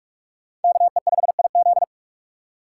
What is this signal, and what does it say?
Morse code, heard as a keyed tone, about 40 wpm, 700 Hz. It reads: KE5IZ